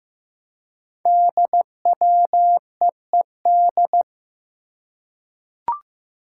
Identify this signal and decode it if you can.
Morse code, heard as a keyed tone, about 15 wpm, 700 Hz. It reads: DWEED
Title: Code Groups - 5 Letter:Numbers - 15wpm